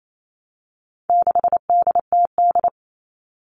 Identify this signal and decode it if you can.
Morse code, heard as a keyed tone, about 28 wpm, 700 Hz. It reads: 6DTD
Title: All Cut Numbers - 4-Digit 28wpm